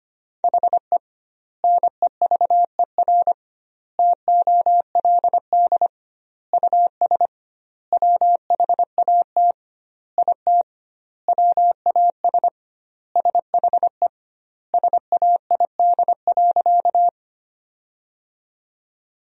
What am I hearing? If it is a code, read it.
Morse code, heard as a keyed tone, about 25 wpm, 700 Hz. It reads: HE NEVER TOLD US WHAT IT WAS SHE SAID.
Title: 2x Word-Spacing - Sentences from Top 400 Words - 25wpm